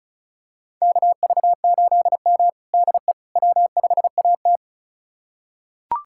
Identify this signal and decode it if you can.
Morse code, heard as a keyed tone, about 34 wpm, 690 Hz. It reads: KV8M DE W5AT
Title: USA Callsign DE Callsign - 4 Characters - 35wpm